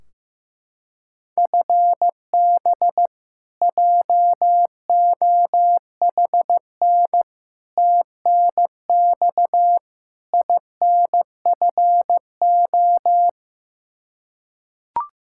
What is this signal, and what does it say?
Morse code, heard as a keyed tone, about 15 wpm, 700 Hz. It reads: FB JOHN TNX INFO
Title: Practice QSO - Exchange 3 - More Info - 15wpm